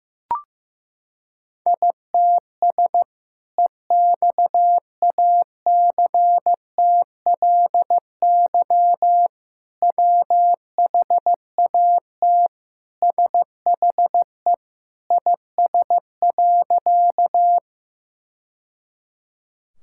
Morse code, heard as a keyed tone, about 15 wpm, 700 Hz. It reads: ITS EXACTLY WHAT SHE IS.
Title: Sentences from Top 1000 Words - Review Part 6 - 15wpm